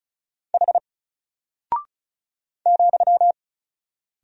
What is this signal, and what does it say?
Morse code, heard as a keyed tone, about 36 wpm, 700 Hz. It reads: H ,
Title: Single Character (Letter:Number:Punctuation:Prosign) Alt 35wpm